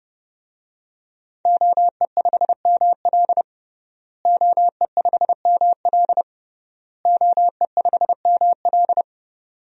Morse code, heard as a keyed tone, about 32 wpm, 700 Hz. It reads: OE5ML OE5ML OE5ML
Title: DXCC-100 Call Signs - 5 Characters - Repeated 3x - 30wpm